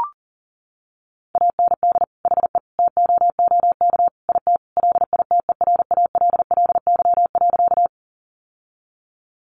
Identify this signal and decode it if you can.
Morse code, heard as a keyed tone, about 40 wpm, 700 Hz. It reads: AND HE TOOK IT LITERALLY.
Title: Sentences from Top 1400 Words 40wpm